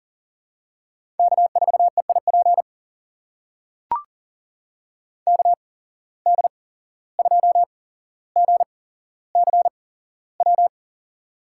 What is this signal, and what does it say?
Morse code, heard as a keyed tone, about 40 wpm, spaced as for 16 wpm, 700 Hz. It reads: K4EIP KD2GCW
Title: USA Call Signs - 8x Character Spacing - 40wpm